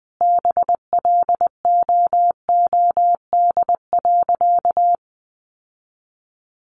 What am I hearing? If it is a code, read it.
Morse code, heard as a keyed tone, about 19 wpm, 700 Hz. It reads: BLOOD.
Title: Sentences from Top 1200 Words 20wpm